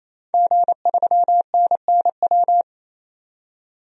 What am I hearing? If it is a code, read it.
Morse code, heard as a keyed tone, about 28 wpm, 700 Hz. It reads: G3NNW
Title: DXCC-100 Call Signs - 5 Characters - 28wpm